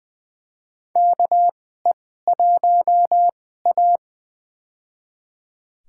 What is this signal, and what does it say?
Morse code, heard as a keyed tone, about 20 wpm, spaced as for 14 wpm, 700 Hz. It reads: KE1A
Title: USA Call Signs - 4 Characters - 2x Character Spacing - 20wpm